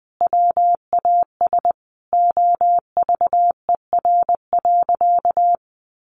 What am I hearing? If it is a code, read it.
Morse code, heard as a keyed tone, about 20 wpm, 700 Hz. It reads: WAS OVER.
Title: Sentences from Top 1500 Words 20wpm